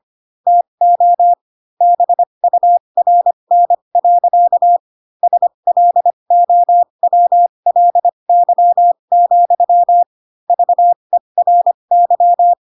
Morse code, heard as a keyed tone, about 25 wpm, spaced as for 22 wpm, 700 Hz. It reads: TO BURN. SLOWLY, VERY
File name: Pinocchio / Pinocchio ch6 18 25 ik1hge